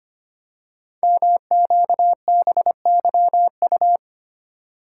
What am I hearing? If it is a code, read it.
Morse code, heard as a keyed tone, about 25 wpm, 700 Hz. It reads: MQBYU
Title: Code Groups - 5 Letters - 25wpm